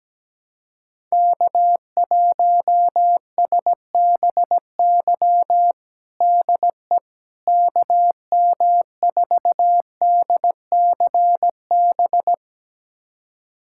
Morse code, heard as a keyed tone, about 17 wpm, 700 Hz. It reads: K1SBY DE KM4DCB
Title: USA Callsign DE Callsign 17wpm